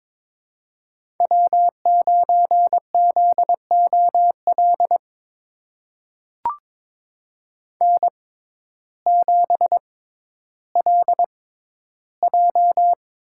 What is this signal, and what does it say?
Morse code, heard as a keyed tone, about 22 wpm, spaced as for 12 wpm, 700 Hz. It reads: W9ZOL N7LJ
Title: USA Call Signs - 6x Character Spacing - 22wpm